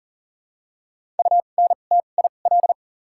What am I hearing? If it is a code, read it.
Morse code, heard as a keyed tone, about 40 wpm, spaced as for 29 wpm, 700 Hz. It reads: UNTIL
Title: 2x Character Spacing - Top 500 Words - 40wpm